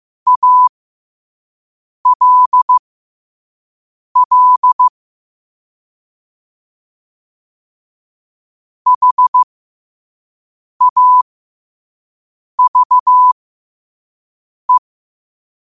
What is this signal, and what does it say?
Morse code, heard as a keyed tone, about 15 wpm, spaced as for 5 wpm, 1000 Hz. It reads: ALL HAVE
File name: qotdfiles / 5wpm